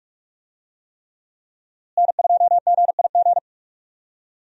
Morse code, heard as a keyed tone, about 46 wpm, 700 Hz. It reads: N2GIG
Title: USA Call Signs - 5 Characters - Repeated 3x - 45wpm